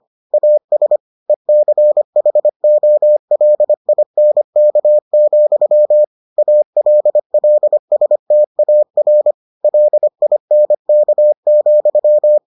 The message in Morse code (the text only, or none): AS ECHOLINK, ALLSTAR LINK,